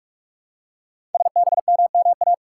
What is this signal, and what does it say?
Morse code, heard as a keyed tone, about 44 wpm, 700 Hz. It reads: SBMMA
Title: Code Groups - 5 Letters - 45wpm